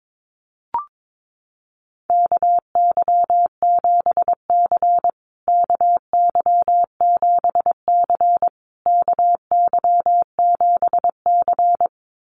Morse code, heard as a keyed tone, about 22 wpm, 700 Hz. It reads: KY7C KY7C KY7C
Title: USA Call Signs Repeated 3x 22wpm